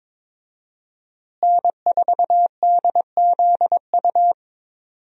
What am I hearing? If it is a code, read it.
Morse code, heard as a keyed tone, about 21 wpm, 700 Hz. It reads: N4DZU
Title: USA Call Signs - 5 Characters - Repeated 2x - 22wpm